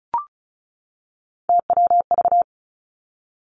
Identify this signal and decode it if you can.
Morse code, heard as a keyed tone, about 36 wpm, 700 Hz. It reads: TWV
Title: Three Challenging Letter-Number Combinations 35wpm